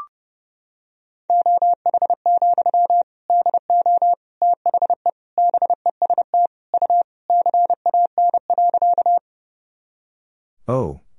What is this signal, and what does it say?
Morse code, heard as a keyed tone, about 32 wpm, 700 Hz. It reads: OH, DO THE BEST U CAN.
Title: Sentences from Top 500 Words and Abbreviations 30wpm